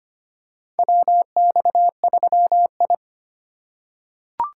Morse code, heard as a keyed tone, about 25 wpm, 700 Hz. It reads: WX3I
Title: USA Call Signs - 4x Character Spacing - 25wpm